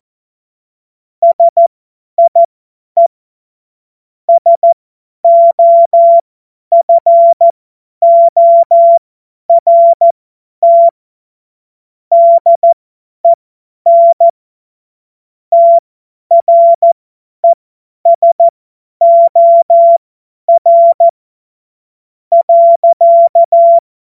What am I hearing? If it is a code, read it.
Morse code, heard as a keyed tone, about 14 wpm, spaced as for 10 wpm, 680 Hz. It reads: SIE SOFORT DEN TRESOR .
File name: Morsekurs / Witze / Witz 23. Mai 2023 50 F70